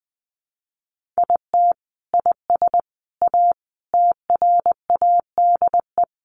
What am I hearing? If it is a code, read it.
Morse code, heard as a keyed tone, about 20 wpm, 700 Hz. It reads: IT IS A TRADE
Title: Sentences from Top 1500 Words 20wpm